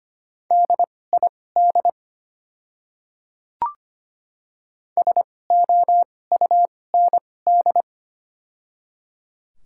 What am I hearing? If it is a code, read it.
Morse code, heard as a keyed tone, about 25 wpm, spaced as for 18 wpm, 700 Hz. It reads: DID SOUND
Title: 2x Character Spacing - Top 200 Words - 25wpm